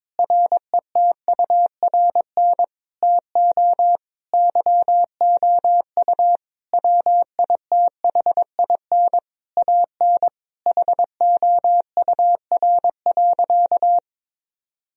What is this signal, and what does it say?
Morse code, heard as a keyed tone, about 22 wpm, 700 Hz. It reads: RETURN TO YOU WITHIN AN HOUR.
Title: Sentences from Top 1000 Words - Review Part 4 - 22wpm